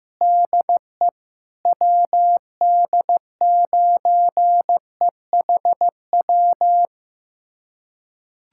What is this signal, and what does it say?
Morse code, heard as a keyed tone, about 16 wpm, 700 Hz. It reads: DE WD9EHW
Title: USA Callsign DE Callsign 15wpm